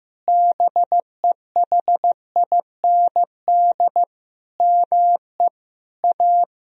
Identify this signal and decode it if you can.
Morse code, heard as a keyed tone, about 15 wpm, 700 Hz. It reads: BEHIND ME A
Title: Sentences from Top 1000 Words - Review Part 2 - 15wpm